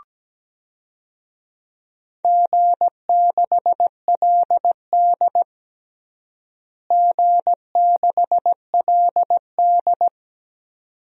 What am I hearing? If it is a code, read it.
Morse code, heard as a keyed tone, about 17 wpm, 700 Hz. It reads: G6LD G6LD
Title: DXCC-100 Call Signs - 4 Characters - Repeated 2x - 17wpm